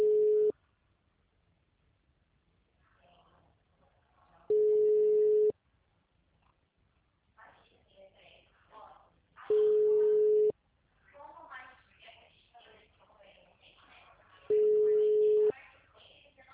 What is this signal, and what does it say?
Ringback tone (Europe, 425 Hz), with speech.